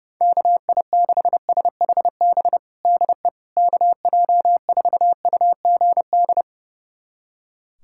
Morse code, heard as a keyed tone, about 32 wpm, 700 Hz. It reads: KI6SHB DE KJ4UGD
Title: USA Callsign DE Callsign 30wpm